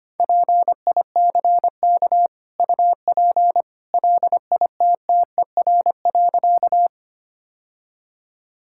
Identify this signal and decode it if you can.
Morse code, heard as a keyed tone, about 25 wpm, 690 Hz. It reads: PICK UP LITTER.
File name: Easy Sentences with Hints - 25wpm